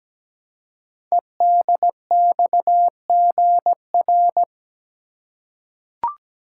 Morse code, heard as a keyed tone, about 17 wpm, 700 Hz. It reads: EDXGR
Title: Code Groups - 5 Letters - 17wpm